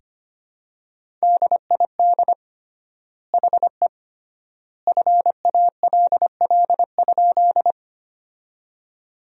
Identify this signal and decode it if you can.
Morse code, heard as a keyed tone, about 25 wpm, 700 Hz. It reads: DID HE FALL?
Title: 3x Word-Spacing - Sentences from Top 500 Words - 25wpm